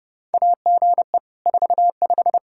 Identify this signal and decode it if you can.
Morse code, heard as a keyed tone, about 30 wpm, 700 Hz. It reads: AGE 45
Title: Practice QSO - Exchange 3 - More Info - 30wpm